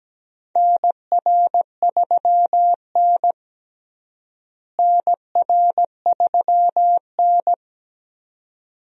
Morse code, heard as a keyed tone, about 17 wpm, 700 Hz. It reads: NR3N NR3N
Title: USA Call Signs - 4 Characters - Repeated 2x - 17wpm